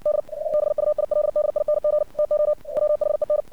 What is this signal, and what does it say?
Morse code, heard as a keyed tone, about 60 wpm, 600 Hz. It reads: D EBUILDING TO WHEN
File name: archived / CRSnet